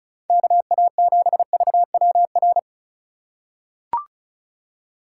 Morse code, heard as a keyed tone, about 36 wpm, 700 Hz. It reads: KA7VWR
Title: USA Call Signs - 6 Characters - 3x Character Spacing - 35wpm